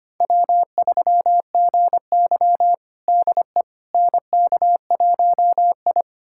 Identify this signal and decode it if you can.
Morse code, heard as a keyed tone, about 25 wpm, 700 Hz. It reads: W3GY DE NK1I
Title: USA Callsign DE Callsign - 4 Characters - 25wpm